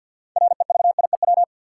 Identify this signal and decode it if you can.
Morse code, heard as a keyed tone, about 51 wpm, 700 Hz. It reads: REVIEW
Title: Sentences from Top 900 Words - New Word Drill - 50wpm